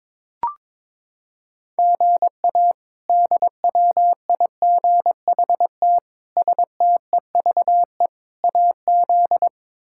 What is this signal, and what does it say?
Morse code, heard as a keyed tone, about 22 wpm, 700 Hz. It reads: GA DWIGHT STEVE AZ
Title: K1USN SST QSO Practice - 22wpm